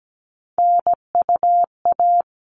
Morse code, heard as a keyed tone, about 17 wpm, 700 Hz. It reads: NUA
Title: All Cut Numbers - 3-Digit 17wpm